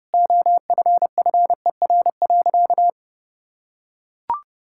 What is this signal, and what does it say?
Morse code, heard as a keyed tone, about 30 wpm, 700 Hz. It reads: OFFER.
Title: Sentences from Top 1600 Words 30wpm